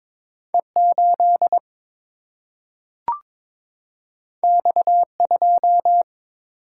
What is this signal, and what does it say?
Morse code, heard as a keyed tone, about 22 wpm, 700 Hz. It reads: E 8 X 2
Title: Single-Letter Single-Number 22wpm